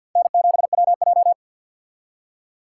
Morse code, heard as a keyed tone, about 60 wpm, 700 Hz. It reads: N7WJ
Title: Speed Racing - USA 1x2 Call Signs 50wpm